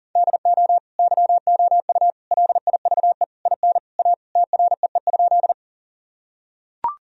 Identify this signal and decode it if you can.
Morse code, heard as a keyed tone, about 37 wpm, 700 Hz. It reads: DO YOU LIVE IN A TREE?
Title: Sentences from Top 1000 Words - Review Part 1 - 40wpm